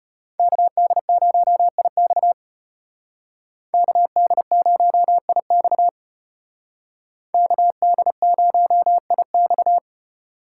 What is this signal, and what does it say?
Morse code, heard as a keyed tone, about 33 wpm, 700 Hz. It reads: KD0IX KD0IX KD0IX